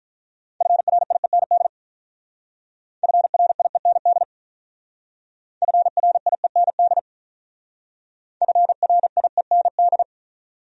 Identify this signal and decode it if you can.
Morse code, heard as a keyed tone, about 44 wpm, 700 Hz. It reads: FRIEND FRIEND FRIEND FRIEND